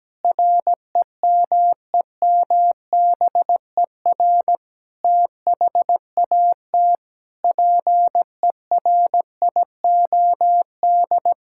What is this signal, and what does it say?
Morse code, heard as a keyed tone, about 17 wpm, 700 Hz. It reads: REMEMBER THAT PERIOD
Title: Sentences from Top 1000 Words - Review Part 5 - 17wpm